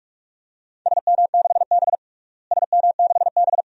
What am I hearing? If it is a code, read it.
Morse code, heard as a keyed tone, about 46 wpm, 700 Hz. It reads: SM6B SM6B
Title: DXCC-100 Call Signs - 4 Characters - Repeated 2x - 45wpm